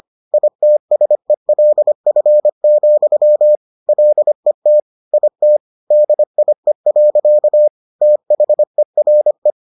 Morse code, heard as a keyed tone, about 25 wpm, 600 Hz. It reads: ITSELF, LET IT DIE. THERE